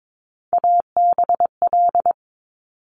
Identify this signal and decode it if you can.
Morse code, heard as a keyed tone, about 22 wpm, 700 Hz. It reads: ABL
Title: Top Three Letter Combinations in Words 22wpm